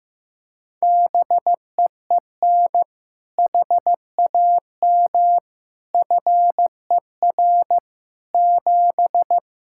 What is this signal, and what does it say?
Morse code, heard as a keyed tone, about 15 wpm, 700 Hz. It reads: BEEN HAM FER 7
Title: K1USN SST QSO Practice 15wpm